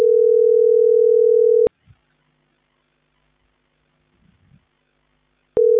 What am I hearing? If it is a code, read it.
Ringback tone (North America), 440 + 480 Hz.